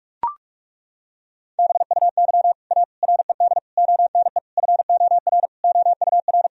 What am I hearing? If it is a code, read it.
Morse code, heard as a keyed tone, about 41 wpm, 700 Hz. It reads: BUY A RED ONE FOR OUR
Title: Sentences from Top 1000 Words - Review Part 2 - 45wpm